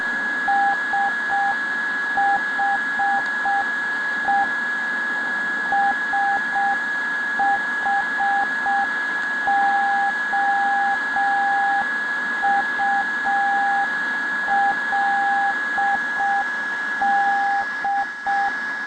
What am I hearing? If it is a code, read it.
Morse code, heard as a keyed tone, about 6 wpm, 800 Hz, with radio noise. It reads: SHE SHOULD